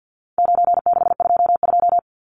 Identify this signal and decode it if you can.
Morse code, heard as a keyed tone, about 50 wpm, 700 Hz. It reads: 9622
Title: Numbers - 4-Digit 50wpm